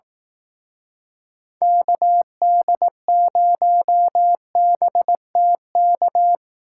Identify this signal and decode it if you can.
Morse code, heard as a keyed tone, about 18 wpm, 700 Hz. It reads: KD0BTK
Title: Speed Racing - USA 2x3 Call Signs 15wpm